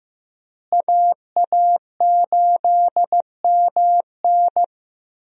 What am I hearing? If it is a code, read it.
Morse code, heard as a keyed tone, about 15 wpm, 700 Hz. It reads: AA8MN
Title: USA Call Signs - 5 Characters - Repeated 3x - 15wpm